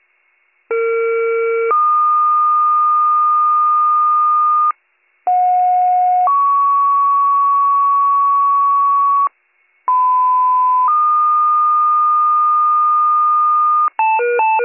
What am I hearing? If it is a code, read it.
Two-tone page, heard as a one-second tone, then a three-second tone, about 460 then 1220 Hz, about 730 then 1120 Hz, about 1010 then 1220 Hz. It is heard from 0.7 to 13.9 s.